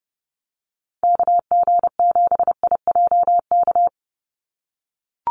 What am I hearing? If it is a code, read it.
Morse code, heard as a keyed tone, about 30 wpm, 700 Hz. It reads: KG7IJK
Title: USA Call Signs 30wpm